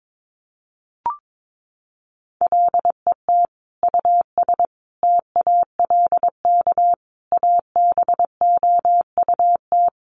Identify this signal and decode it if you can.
Morse code, heard as a keyed tone, about 22 wpm, 700 Hz. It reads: LET US TALK ABOUT
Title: Sentences from Top 1100 Words 22wpm